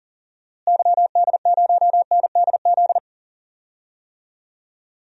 Morse code, heard as a keyed tone, about 38 wpm, 700 Hz. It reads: YD0NDZ
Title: DXCC-100 Call Signs - 6 Characters - Repeated 3x - 40wpm